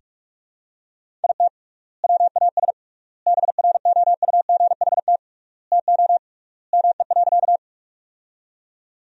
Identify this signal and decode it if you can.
Morse code, heard as a keyed tone, about 41 wpm, 700 Hz. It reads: IT WAS BROUGHT TO ME.
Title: 3x Word-Spacing - Sentences from Top 500 Words - 45wpm